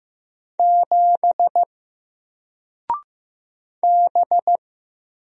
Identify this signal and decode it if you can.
Morse code, heard as a keyed tone, about 15 wpm, 700 Hz. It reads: 7 B
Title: Single Character (Letter:Number:Punctuation:Prosign) 15wpm